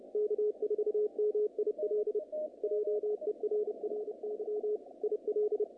Morse code, heard as a keyed tone, about 29 wpm, 440 Hz. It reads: K4MIL JERRY IL
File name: contests / naqp-cw-k5tr-op-k5pi- -2200